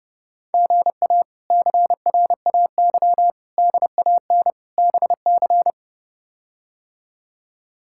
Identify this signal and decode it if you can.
Morse code, heard as a keyed tone, about 30 wpm, 700 Hz. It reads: GA CRAY DAN BC
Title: K1USN SST QSO Practice - 30wpm